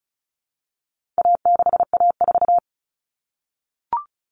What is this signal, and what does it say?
Morse code, heard as a keyed tone, about 34 wpm, 700 Hz. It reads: A6A4
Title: Common Cut Numbers - 4-Digit 35wpm